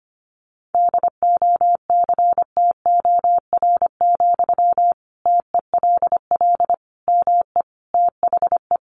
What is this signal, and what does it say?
Morse code, heard as a keyed tone, about 25 wpm, 700 Hz. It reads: DOCTOR, TELL ME THE
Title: Sentences from Top 1500 Words 25wpm